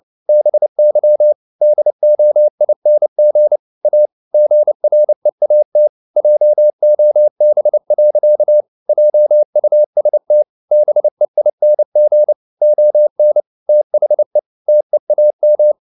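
Morse code, heard as a keyed tone, about 29 wpm, 600 Hz. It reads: BY DOING A GREAT JOB. JUST BEING ON THE TEAM
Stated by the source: Morse code at 29 wpm, sending Ken Follett's novel Never in English, from CW med Gnister.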